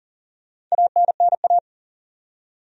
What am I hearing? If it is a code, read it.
Morse code, heard as a keyed tone, about 40 wpm, 700 Hz. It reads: ANNA
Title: USA Common Names - 40wpm